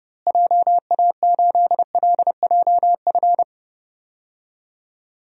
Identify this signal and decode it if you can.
Morse code, heard as a keyed tone, about 29 wpm, 700 Hz. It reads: JA8LJF
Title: DXCC-100 Call Signs - 6 Characters - Repeated 3x - 30wpm